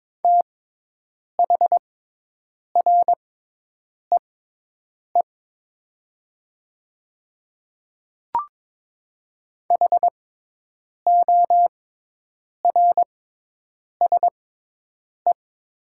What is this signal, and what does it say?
Morse code, heard as a keyed tone, about 22 wpm, spaced as for 8 wpm, 700 Hz. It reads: THREE HORSE
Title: 6x Character Spacing - Top 300 Words - 22wpm